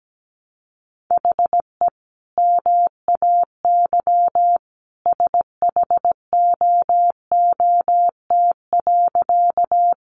Morse code, heard as a keyed tone, about 17 wpm, 700 Hz. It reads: HE MAY SHOOT.
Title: Sentences from Top 1200 Words 17wpm